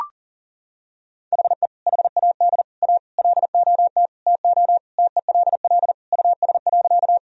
Morse code, heard as a keyed tone, about 40 wpm, 700 Hz. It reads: HE HAD A LOT TO TELL US.
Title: Sentences from Top 1000 Words - Review Part 3 - 40wpm